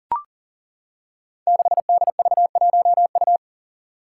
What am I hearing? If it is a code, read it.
Morse code, heard as a keyed tone, about 39 wpm, 700 Hz. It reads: 6DV1U
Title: Five Challenging Letter-Number Combinations 40wpm